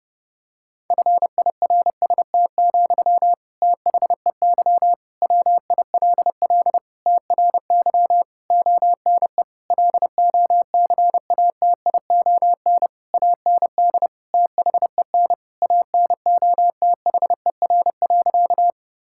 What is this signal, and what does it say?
Morse code, heard as a keyed tone, about 30 wpm, 700 Hz. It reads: FIRST, THEY WILL TRY ONE LOCATION AND THEN ANOTHER.
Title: Sentences from Top 1000 Words - Review Part 6 - 30wpm